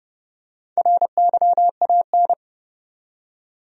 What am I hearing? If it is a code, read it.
Morse code, heard as a keyed tone, about 30 wpm, 700 Hz. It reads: RYAN
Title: USA Common Names - Max 4 Letters - 30wpm